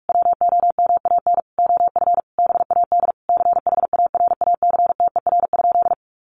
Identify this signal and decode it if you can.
Morse code, heard as a keyed tone, about 44 wpm, 700 Hz. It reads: WOMAN OF BAD CHARACTER?
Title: Sentences from Top 1200 Words 45wpm